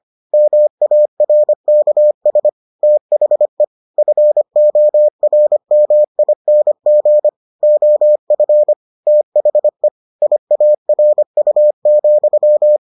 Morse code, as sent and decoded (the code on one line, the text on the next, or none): -- .- .-. -.- ... / - .... / ..-. --- .-. -- .. -. --. / --- ..-. / - .... / .. .- .-. ..- --..--
MARKS THE FORMING OF THE IARU,